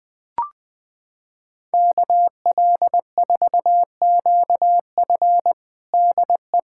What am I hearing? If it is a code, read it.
Morse code, heard as a keyed tone, about 20 wpm, 700 Hz. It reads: KL4QF DE
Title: USA Callsign DE Callsign 20wpm